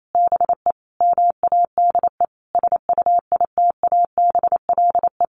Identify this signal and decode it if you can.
Morse code, heard as a keyed tone, about 28 wpm, 700 Hz. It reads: BE MADE SUITABLE
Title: Sentences from Top 1400 Words 28wpm